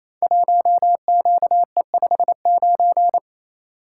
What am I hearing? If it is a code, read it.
Morse code, heard as a keyed tone, about 29 wpm, 700 Hz. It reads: 1QE59
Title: Code Groups - 5 Letter:Numbers - 28wpm